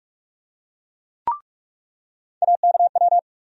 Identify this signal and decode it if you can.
Morse code, heard as a keyed tone, about 46 wpm, 700 Hz. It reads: AKW
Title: USA Call Sign Postfixes - 3 Letters 45wpm